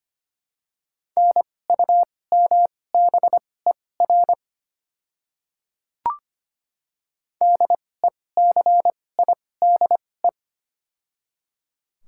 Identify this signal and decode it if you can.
Morse code, heard as a keyed tone, about 25 wpm, spaced as for 18 wpm, 700 Hz. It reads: NUMBER DECIDE
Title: 2x Character Spacing - Top 500 Words - 25wpm